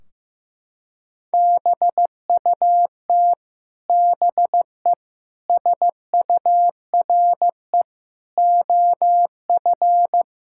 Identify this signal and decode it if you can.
Morse code, heard as a keyed tone, about 15 wpm, 700 Hz. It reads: BUT BE SURE OF
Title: Sentences from Top 1600 Words 15wpm